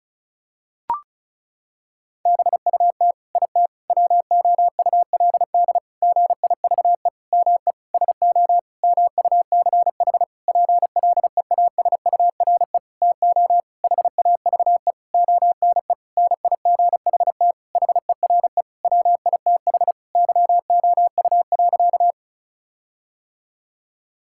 Morse code, heard as a keyed tone, about 36 wpm, 700 Hz. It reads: BUT IT WOULD GIVE ME SO MUCH PLEASURE TO HAVE ONE NIGHT HERE WITH YOU.
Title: Sentences from Top 1600 Words 35wpm